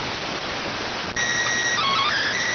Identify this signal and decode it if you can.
SSTV (slow-scan television), PD 290.